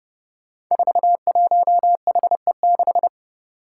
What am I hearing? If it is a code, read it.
Morse code, heard as a keyed tone, about 30 wpm, 700 Hz. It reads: 41HE6
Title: Five Challenging Letter-Number Combinations 30wpm